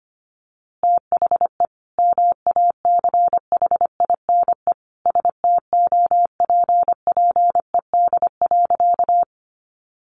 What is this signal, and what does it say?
Morse code, heard as a keyed tone, about 25 wpm, 700 Hz. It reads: THE MACHINE STOPPED.
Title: A Sound Of Thunder 25wpm